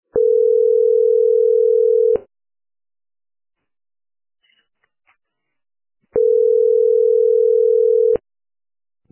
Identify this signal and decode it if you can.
Ringback tone (North America), 440 + 480 Hz.